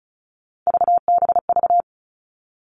Morse code, heard as a keyed tone, about 34 wpm, 700 Hz. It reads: VBV
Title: All Cut Numbers - 3-Digit 35wpm